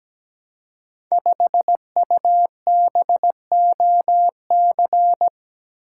Morse code, heard as a keyed tone, about 17 wpm, 700 Hz. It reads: 5UBOC
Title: Code Groups - 5 Letter:Numbers - 17wpm